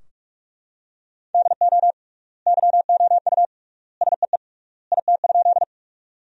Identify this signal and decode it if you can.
Morse code, heard as a keyed tone, about 45 wpm, 700 Hz. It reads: DO YOU SEE IT?
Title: 3x Word-Spacing - Sentences from Top 100 Words - 45wpm